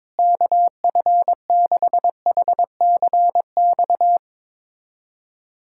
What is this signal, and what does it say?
Morse code, heard as a keyed tone, about 22 wpm, 700 Hz. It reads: KF6HCX
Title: USA Callsign DE Callsign - 6 Characters - 22wpm